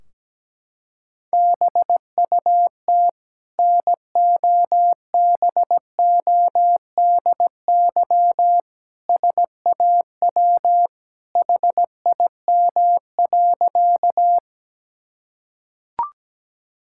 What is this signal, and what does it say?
Morse code, heard as a keyed tone, about 17 wpm, 700 Hz. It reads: BUT NOBODY SAW HIM.
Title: Sentences from Top 1600 Words 17wpm